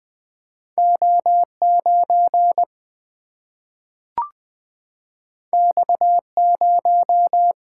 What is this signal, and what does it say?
Morse code, heard as a keyed tone, about 20 wpm, 700 Hz. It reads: O9 X0
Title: Single-Letter Single-Number 20wpm